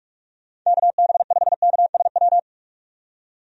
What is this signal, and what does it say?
Morse code, heard as a keyed tone, about 47 wpm, 700 Hz. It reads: KB5KSW
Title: USA Call Signs - 6 Characters - 45wpm